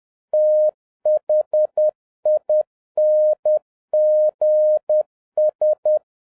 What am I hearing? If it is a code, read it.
Morse code, heard as a keyed tone, about 10 wpm, 610 Hz. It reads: THINGS
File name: 2011-07-mp3 / n36 THE 3.11 2 10wpm